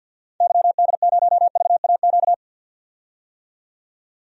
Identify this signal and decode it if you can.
Morse code, heard as a keyed tone, about 49 wpm, 700 Hz. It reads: YD0VAQ
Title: DXCC-100 Call Signs - 6 Characters - 50wpm